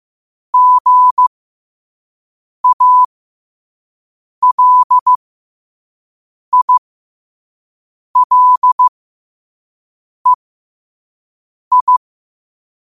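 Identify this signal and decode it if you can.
Morse code, heard as a keyed tone, about 15 wpm, spaced as for 5 wpm, 1000 Hz. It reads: GALILEI
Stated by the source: Morse code quote-of-the-day practice at 5 wpm.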